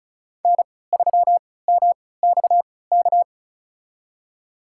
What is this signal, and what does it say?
Morse code, heard as a keyed tone, about 36 wpm, spaced as for 20 wpm, 700 Hz. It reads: N3MXK
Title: USA Call Signs - 5 Characters - 3x Character Spacing - 35wpm